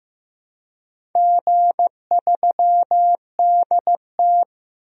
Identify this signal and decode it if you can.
Morse code, heard as a keyed tone, about 15 wpm, 700 Hz. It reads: G3DT
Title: DXCC-100 Call Signs - 4 Characters - 15wpm